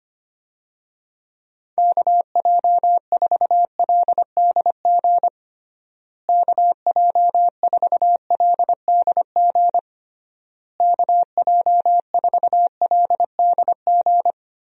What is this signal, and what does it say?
Morse code, heard as a keyed tone, about 25 wpm, 700 Hz. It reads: KJ4LDG KJ4LDG KJ4LDG
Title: USA Call Signs - 6 Characters - Repeated 3x - 25wpm